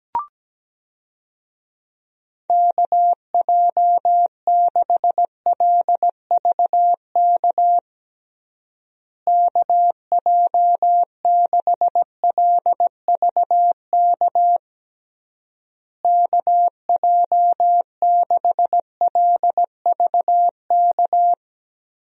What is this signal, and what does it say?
Morse code, heard as a keyed tone, about 17 wpm, 700 Hz. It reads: KJ6LVK KJ6LVK KJ6LVK
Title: USA Call Signs - 6 Characters - Repeated 3x - 17wpm